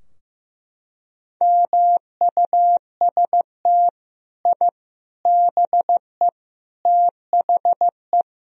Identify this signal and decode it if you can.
Morse code, heard as a keyed tone, about 16 wpm, 700 Hz. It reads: MUST I BE THE
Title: Sentences from Top 1000 Words - Review Part 4 - 15wpm